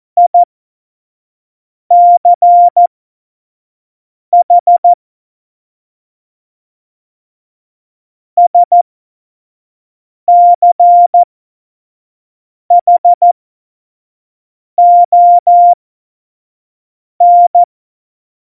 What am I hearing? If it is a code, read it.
Morse code, heard as a keyed tone, about 14 wpm, spaced as for 5 wpm, 700 Hz. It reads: ICH SCHON
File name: Morsekurs / Witze / Witz 26. Februar 2019, 25 F70